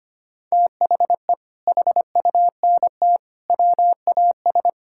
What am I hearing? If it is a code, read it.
Morse code, heard as a keyed tone, about 25 wpm, 700 Hz. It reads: THE HUNT WAS
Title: Sentences from Top 1600 Words 25wpm